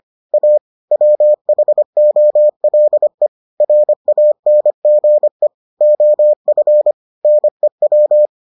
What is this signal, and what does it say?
Morse code, heard as a keyed tone, about 25 wpm, 600 Hz. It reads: A WHOLE RANGE OF NEW